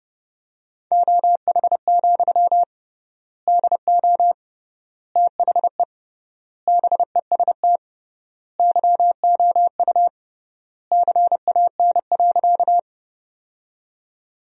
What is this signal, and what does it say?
Morse code, heard as a keyed tone, about 28 wpm, 700 Hz. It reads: OH, DO THE BEST YOU CAN.
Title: 3x Word-Spacing - Sentences from Top 500 Words - 30wpm